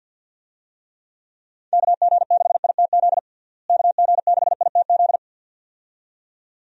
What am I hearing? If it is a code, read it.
Morse code, heard as a keyed tone, about 50 wpm, 700 Hz. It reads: KG6ITZ KG6ITZ
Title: USA Call Signs - 6 Characters - Repeated 2x - 50wpm